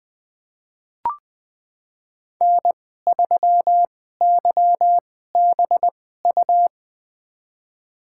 Morse code, heard as a keyed tone, about 20 wpm, spaced as for 15 wpm, 700 Hz. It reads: N3YBU
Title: USA Call Signs - 5 Characters - 2x Character Spacing - 20wpm